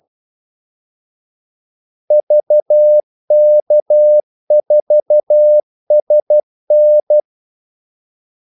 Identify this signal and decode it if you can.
Morse code, heard as a keyed tone, about 12 wpm, 600 Hz. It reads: VK4SN
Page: 2 ekstra ordmellemrum mellem kaldesignalerne